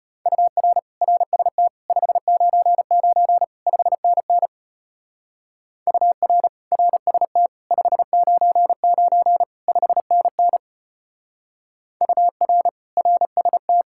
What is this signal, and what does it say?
Morse code, heard as a keyed tone, about 32 wpm, 700 Hz. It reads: UR RST 599 5NN UR RST 599 5NN UR RST